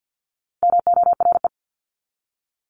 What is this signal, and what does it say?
Morse code, heard as a keyed tone, about 48 wpm, 700 Hz. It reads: MORE
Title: Top 300 Words 50wpm